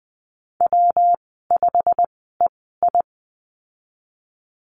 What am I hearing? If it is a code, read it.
Morse code, heard as a keyed tone, about 20 wpm, spaced as for 15 wpm, 700 Hz. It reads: W5EI
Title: 147 Lesson 24g - Letter y - Call Signs 20wpm